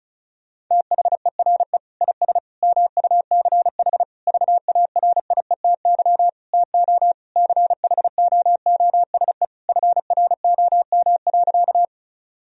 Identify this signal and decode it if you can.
Morse code, heard as a keyed tone, about 36 wpm, 700 Hz. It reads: THERE IS MUCH VARIETY TO CHOOSE FROM.
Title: Sentences from Top 1000 Words - Review Part 4 - 35wpm